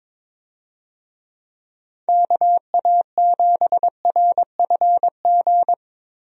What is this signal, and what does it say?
Morse code, heard as a keyed tone, about 22 wpm, 700 Hz. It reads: KA7RFG
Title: USA Call Signs - 6 Characters - Repeated 2x - 22wpm